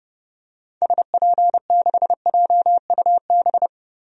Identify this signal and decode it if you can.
Morse code, heard as a keyed tone, about 30 wpm, 690 Hz. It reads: SP6JUB
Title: DXCC-100 Call Signs - 6 Characters - 30wpm